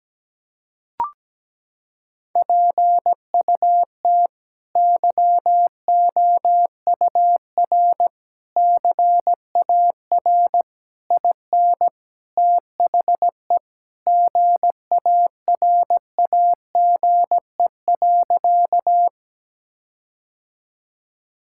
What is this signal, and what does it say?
Morse code, heard as a keyed tone, about 17 wpm, 700 Hz. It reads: PUT YOUR CAR IN THE GARAGE.
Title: Sentences from Top 1600 Words 17wpm